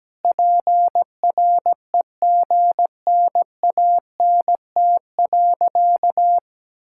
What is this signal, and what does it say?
Morse code, heard as a keyed tone, about 17 wpm, 700 Hz. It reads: PREGNANT.
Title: Sentences from Top 1000 Words - Review Part 6 - 17wpm